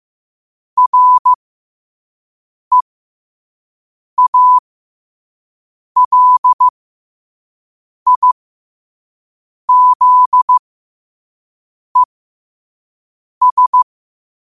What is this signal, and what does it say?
Morse code, heard as a keyed tone, about 15 wpm, spaced as for 5 wpm, 1000 Hz. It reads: REALIZES
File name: qotdfiles / 5wpm